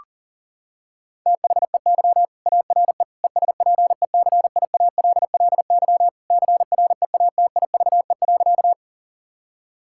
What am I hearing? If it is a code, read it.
Morse code, heard as a keyed tone, about 37 wpm, 700 Hz. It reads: THEY ARE ESPECIALLY CREATIVE.
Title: Sentences from Top 1000 Words - Review Part 5 - 40wpm